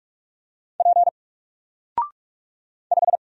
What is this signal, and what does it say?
Morse code, heard as a keyed tone, about 46 wpm, 700 Hz. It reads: P 5
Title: Single Letter-Number Alt 45wpm